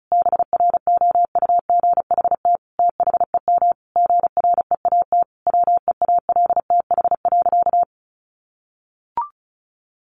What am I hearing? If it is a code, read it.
Morse code, heard as a keyed tone, about 34 wpm, 700 Hz. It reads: BROUGHT THEM GREAT WEALTH.
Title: Sentences from Top 1300 Words 35wpm